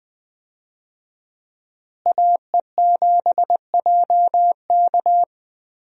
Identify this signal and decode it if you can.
Morse code, heard as a keyed tone, about 20 wpm, 700 Hz. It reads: AE7JK
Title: USA Call Signs - 5 Characters - Repeated 2x - 20wpm